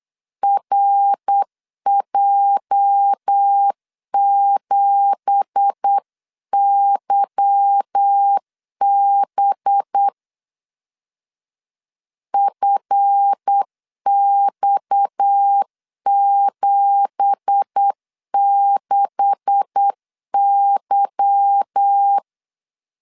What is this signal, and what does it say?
Morse code, heard as a keyed tone, about 8 wpm, 800 Hz. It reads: RJ7YB FX76Y
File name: LEZIONI / 9 2867 8wpm 3440